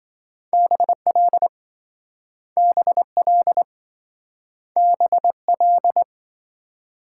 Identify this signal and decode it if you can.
Morse code, heard as a keyed tone, about 27 wpm, 700 Hz. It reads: BL BL BL